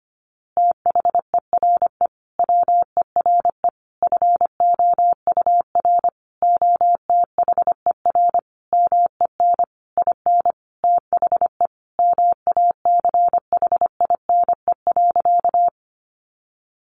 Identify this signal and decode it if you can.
Morse code, heard as a keyed tone, about 25 wpm, 700 Hz. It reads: THERE WERE FOUR OTHER MEN IN THE MACHINE.
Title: A Sound Of Thunder 25wpm